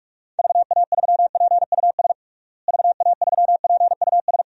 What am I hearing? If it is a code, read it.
Morse code, heard as a keyed tone, about 46 wpm, 700 Hz. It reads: VA3PUS VA3PUS
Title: DXCC-100 Call Signs - 6 Characters - Repeated 2x - 45wpm